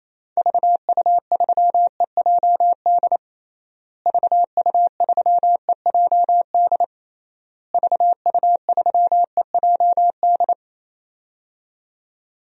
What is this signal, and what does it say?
Morse code, heard as a keyed tone, about 27 wpm, 700 Hz. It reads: VU3EJD VU3EJD VU3EJD
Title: DXCC-100 Call Signs - 6 Characters - Repeated 3x - 28wpm